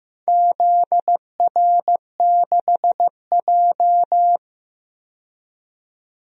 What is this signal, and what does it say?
Morse code, heard as a keyed tone, about 15 wpm, 700 Hz. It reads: ZR6J
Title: DXCC-100 Call Signs - 4 Characters - 15wpm